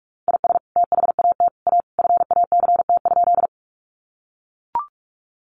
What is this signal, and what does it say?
Morse code, heard as a keyed tone, about 44 wpm, 700 Hz. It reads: IS THAT A FACT?
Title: Sentences from Top 500 Words 45wpm